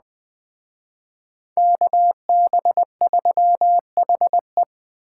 Morse code, heard as a keyed tone, about 20 wpm, 700 Hz. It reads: KB3HE
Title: Speed Racing - USA 2x2 Call Signs 17wpm